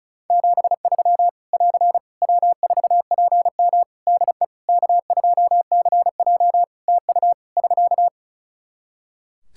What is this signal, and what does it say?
Morse code, heard as a keyed tone, about 34 wpm, 700 Hz. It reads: 73 <AR> W4PM DE K2CJ TU <SK>
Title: K1USN SST QSO Practice 35wpm